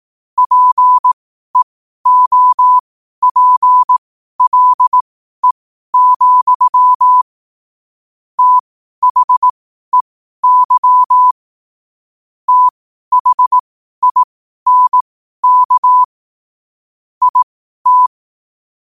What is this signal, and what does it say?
Morse code, heard as a keyed tone, about 18 wpm, spaced as for 12 wpm, 1000 Hz. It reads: PEOPLE, THEY THINK IT
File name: qotdfiles / 12wpm